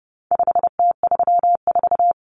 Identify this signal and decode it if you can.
Morse code, heard as a keyed tone, about 30 wpm, 700 Hz. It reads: 5T34
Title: Common Cut Numbers - 4-Digit 30wpm